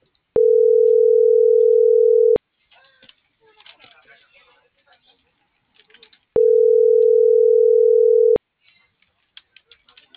Ringback tone (North America), 440 + 480 Hz.